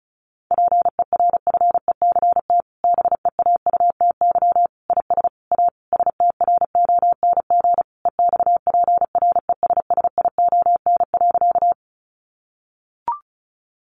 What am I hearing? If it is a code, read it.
Morse code, heard as a keyed tone, about 34 wpm, 700 Hz. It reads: PERFECT BEAUTY IS A STRONG EXPRESSION.
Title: Sentences from Top 1200 Words 35wpm